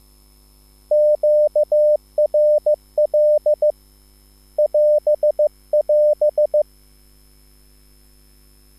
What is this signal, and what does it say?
Morse code, heard as a keyed tone, about 15 wpm, 600 Hz. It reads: QRL <AS><AS>